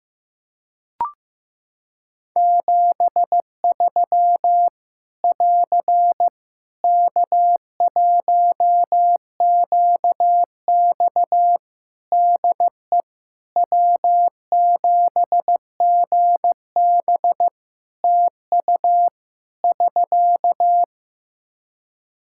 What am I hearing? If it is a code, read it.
Morse code, heard as a keyed tone, about 15 wpm, 700 Hz. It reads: 73 <AR> K1QX DE W7GB TU <SK>
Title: K1USN SST QSO Practice 15wpm